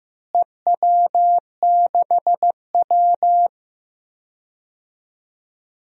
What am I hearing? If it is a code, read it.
Morse code, heard as a keyed tone, about 15 wpm, 700 Hz. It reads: EW6W
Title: DXCC-100 Call Signs - 4 Characters - Repeated 2x - 15wpm